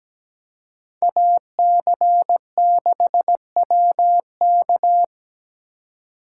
Morse code, heard as a keyed tone, about 20 wpm, 700 Hz. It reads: AC6WK
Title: Speed Racing - USA 2x2 Call Signs 17wpm